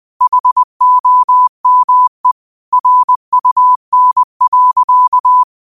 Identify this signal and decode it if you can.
Morse code, heard as a keyed tone, about 20 wpm, 1000 Hz. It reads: HOME RUN.
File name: qotdfiles / 20wpm